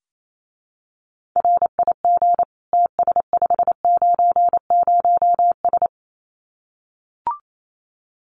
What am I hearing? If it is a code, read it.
Morse code, heard as a keyed tone, about 28 wpm, 700 Hz. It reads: RIG TS590S
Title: Common QSO Phrases Encore 28wpm